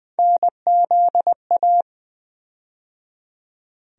Morse code, heard as a keyed tone, about 20 wpm, 700 Hz. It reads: NZA
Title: USA Call Sign Postfixes - 3 Letters 20wpm